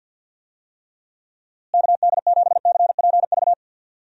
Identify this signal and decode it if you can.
Morse code, heard as a keyed tone, about 51 wpm, 700 Hz. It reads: KD7CPV
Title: USA Call Signs - 6 Characters - Repeated 2x - 50wpm